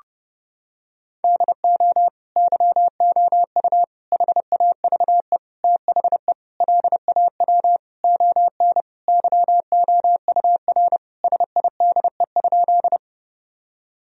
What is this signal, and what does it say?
Morse code, heard as a keyed tone, about 30 wpm, 700 Hz. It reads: DO YOU HAVE THE LAW ON YOUR SIDE?
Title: Sentences from Top 1000 Words - Review Part 3 - 30wpm